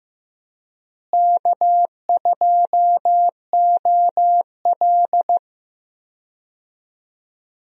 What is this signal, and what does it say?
Morse code, heard as a keyed tone, about 15 wpm, 700 Hz. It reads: K2OL
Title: USA Call Signs - 4 Characters - Repeated 3x - 15wpm